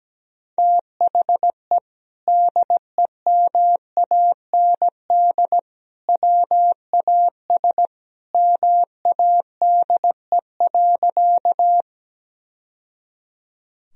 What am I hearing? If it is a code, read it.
Morse code, heard as a keyed tone, about 17 wpm, 700 Hz. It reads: THE DEMAND WAS MADE.
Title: Sentences from Top 1000 Words - Review Part 4 - 17wpm